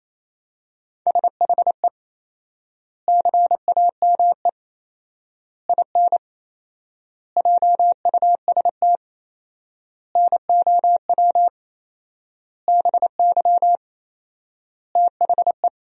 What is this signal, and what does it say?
Morse code, heard as a keyed tone, about 28 wpm, 700 Hz. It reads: SHE CAME IN JUST NOW BY THE
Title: 4x Word-Spacing - Sentences from Top 300 Words - 28wpm